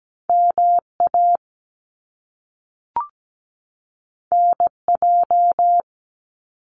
Morse code, heard as a keyed tone, about 17 wpm, 700 Hz. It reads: MA NJ
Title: US States 17wpm